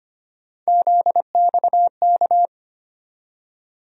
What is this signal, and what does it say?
Morse code, heard as a keyed tone, about 25 wpm, 700 Hz. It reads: ZXK
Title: USA Call Sign Postfixes - 3 Letters 25wpm